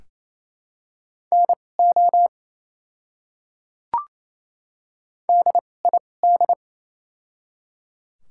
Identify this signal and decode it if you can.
Morse code, heard as a keyed tone, about 28 wpm, spaced as for 20 wpm, 700 Hz. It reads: NO DID
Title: 2x Character Spacing - Top 100 Words - 28wpm